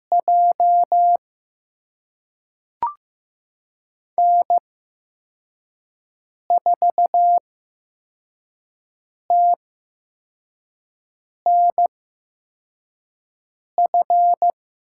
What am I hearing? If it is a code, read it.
Morse code, heard as a keyed tone, about 15 wpm, spaced as for 4 wpm, 700 Hz. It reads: J N4TNF
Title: USA Call Signs - 8x Character Spacing - 15wpm